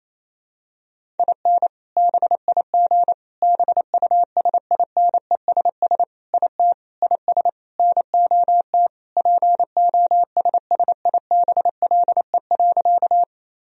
Morse code, heard as a keyed tone, about 28 wpm, 700 Hz. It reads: IN BIG BUSINESS IT IS NOT POSSIBLE.
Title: Sentences from Top 1000 Words - Review Part 1 - 28wpm